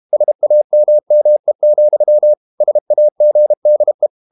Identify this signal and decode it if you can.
Morse code, heard as a keyed tone, about 32 wpm, 600 Hz. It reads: SAMME, SAGDE
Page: Ferie 32 wpm